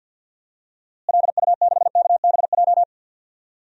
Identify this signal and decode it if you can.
Morse code, heard as a keyed tone, about 52 wpm, 700 Hz. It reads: LU6KBJ
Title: DXCC-100 Call Signs - 6 Characters - Repeated 3x - 50wpm